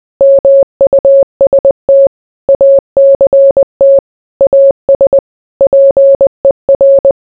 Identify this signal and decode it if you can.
Morse code, heard as a keyed tone, about 20 wpm, 550 Hz. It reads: MUST ACT AS PER